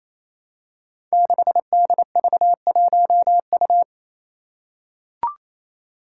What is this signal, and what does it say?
Morse code, heard as a keyed tone, about 27 wpm, 700 Hz. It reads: 6DV1U
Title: Five Challenging Letter-Number Combinations 28wpm